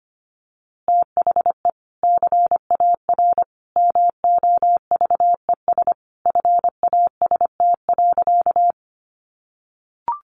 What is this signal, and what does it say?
Morse code, heard as a keyed tone, about 25 wpm, 700 Hz. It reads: THE CAR MOVES FAST.
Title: Easy Phrases 25wpm